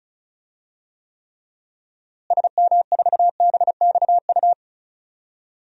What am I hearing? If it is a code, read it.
Morse code, heard as a keyed tone, about 34 wpm, 700 Hz. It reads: SM4BXU
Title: DXCC-100 Call Signs - 6 Characters - Repeated 3x - 35wpm